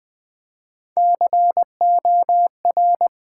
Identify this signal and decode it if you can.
Morse code, heard as a keyed tone, about 20 wpm, 700 Hz. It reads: COR
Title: Build A Word - 3 Letters At A Time - 9-Letter Words - 20wpm